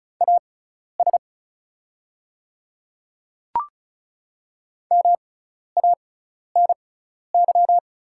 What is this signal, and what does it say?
Morse code, heard as a keyed tone, about 36 wpm, spaced as for 12 wpm, 690 Hz. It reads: AS MANY
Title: 6x Character Spacing - Top 100 Words - 35wpm